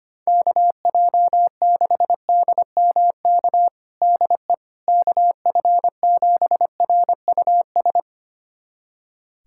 Morse code, heard as a keyed tone, about 25 wpm, 700 Hz. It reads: KJ6DMK DE KF7RUS
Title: USA Callsign DE Callsign - 6 Characters - 25wpm